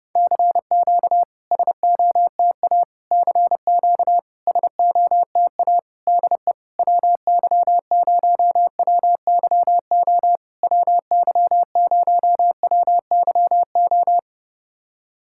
Morse code, heard as a keyed tone, about 30 wpm, 700 Hz. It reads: CQ SOTA CQ SOTA DE WY0WYO WY0WYO
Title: SOTA QSO Practice 30wpm